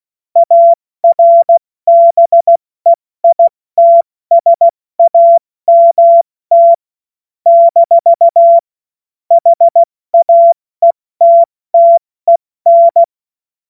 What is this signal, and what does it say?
Morse code, heard as a keyed tone, about 16 wpm, 680 Hz. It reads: ARBEITSAMT - HAETTEN
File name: Morsekurs / Witze / Witz 29. November 2022 70 F80